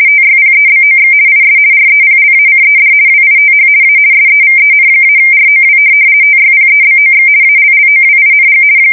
RTTY, about 50 baud. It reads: TYPEBOX IS REPLACED WITH A SEVEN-ROW TYPEBOX ALLOWING
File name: fsk170-50B / teletype-corp